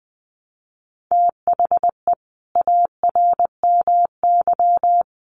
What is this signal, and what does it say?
Morse code, heard as a keyed tone, about 20 wpm, 700 Hz. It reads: THE ARMY
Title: Sentences from Top 800 Words 20wpm